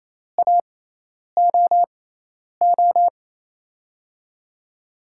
Morse code, heard as a keyed tone, about 28 wpm, spaced as for 10 wpm, 700 Hz. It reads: AOO
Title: USA Call Signs - 6x Character Spacing - 28wpm